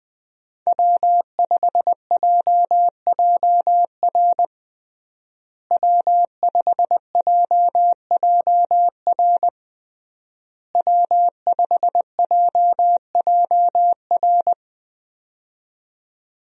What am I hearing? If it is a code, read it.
Morse code, heard as a keyed tone, about 20 wpm, 700 Hz. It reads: W5JJR W5JJR W5JJR
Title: USA Call Signs - 5 Characters - Repeated 3x - 20wpm